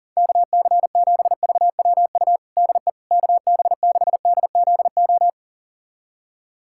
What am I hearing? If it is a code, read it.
Morse code, heard as a keyed tone, about 40 wpm, 700 Hz. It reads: KC7VWU DE KB6DZO
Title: USA Callsign DE Callsign 40wpm